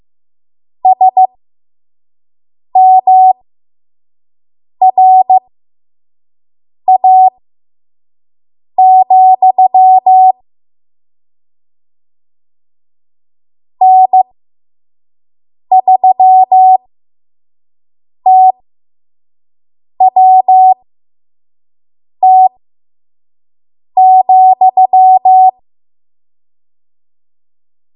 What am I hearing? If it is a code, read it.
Morse code, heard as a keyed tone, about 15 wpm, spaced as for 5 wpm, 750 Hz. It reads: SMRA, N3TWT,
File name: Morse / 05WPM